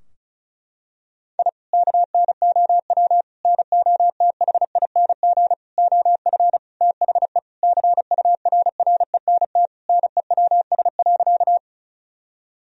Morse code, heard as a keyed tone, about 36 wpm, 700 Hz. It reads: I KNOW NOTHING OF THE CURRENT NEWS.
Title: Sentences from Top 1000 Words - Review Part 4 - 35wpm